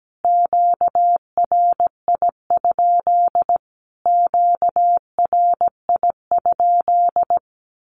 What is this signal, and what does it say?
Morse code, heard as a keyed tone, about 17 wpm, 700 Hz. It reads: QRI? QRI?
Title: Q-Codes Repeated 3x 17wpm